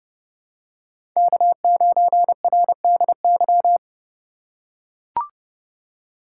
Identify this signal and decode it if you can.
Morse code, heard as a keyed tone, about 32 wpm, 700 Hz. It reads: K9RDY
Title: USA Call Signs - 5 Characters - 3x Character Spacing - 30wpm